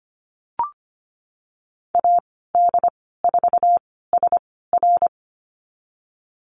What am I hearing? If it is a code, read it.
Morse code, heard as a keyed tone, about 25 wpm, spaced as for 16 wpm, 700 Hz. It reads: AD4SR
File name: Lesson 15g - Number 5 - Call Signs 25wpm